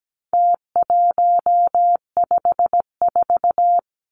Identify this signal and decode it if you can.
Morse code, heard as a keyed tone, about 17 wpm, 700 Hz. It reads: T154
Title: Numbers - Contest Style - 4-Digit 17wpm